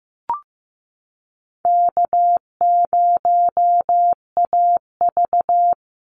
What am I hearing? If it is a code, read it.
Morse code, heard as a keyed tone, about 15 wpm, 700 Hz. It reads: K0AV
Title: Easy Call Signs Repeated 3x - 15wpm